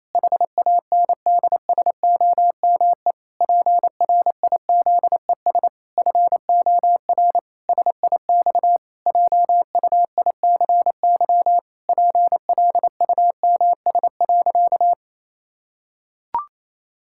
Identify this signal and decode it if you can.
Morse code, heard as a keyed tone, about 28 wpm, 700 Hz. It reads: HANDSOME PRIZES FOR SIX JUICY PLUMS.
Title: Pangram Challenge v2 - 28wpm